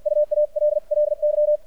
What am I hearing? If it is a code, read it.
Morse code, heard as a keyed tone, about 55 wpm, spaced as for 45 wpm, 600 Hz. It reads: UALLY
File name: archived / CRSnet